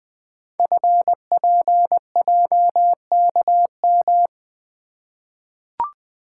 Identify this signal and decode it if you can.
Morse code, heard as a keyed tone, about 20 wpm, 700 Hz. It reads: FPJKM
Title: Code Groups - 5 Letter:Numbers - 20wpm